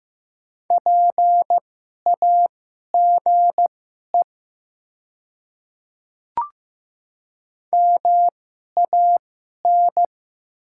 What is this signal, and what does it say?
Morse code, heard as a keyed tone, about 15 wpm, spaced as for 11 wpm, 700 Hz. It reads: PAGE MAN
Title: 2x Character Spacing - Top 300 Words - 15wpm